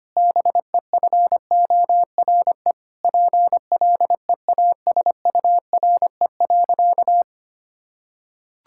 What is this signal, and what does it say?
Morse code, heard as a keyed tone, about 25 wpm, 700 Hz. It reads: BEFORE PLEASURE.
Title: Sentences from Top 1600 Words 25wpm